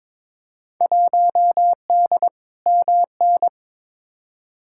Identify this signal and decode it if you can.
Morse code, heard as a keyed tone, about 22 wpm, 700 Hz. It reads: 1D MN
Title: Field Day Exchanges - 22wpm